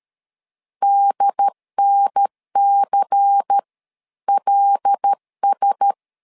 Morse code, heard as a keyed tone, about 13 wpm, spaced as for 11 wpm, 800 Hz. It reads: DNCLS
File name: LEZIONI / 5 LBVC / 5 LBVC 12wpm 1955